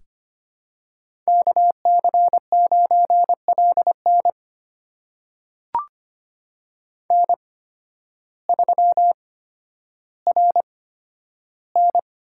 Morse code, heard as a keyed tone, about 25 wpm, spaced as for 12 wpm, 700 Hz. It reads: KC9LN N3RN
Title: USA Call Signs - 5 Characters - 8x Character Spacing - 25wpm